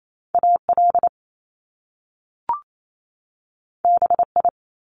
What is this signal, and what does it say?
Morse code, heard as a keyed tone, about 28 wpm, 700 Hz. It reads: AL BI
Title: Top Two Letter Combinations in Words 28wpm